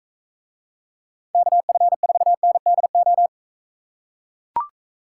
Morse code, heard as a keyed tone, about 44 wpm, 700 Hz. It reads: KF4NDO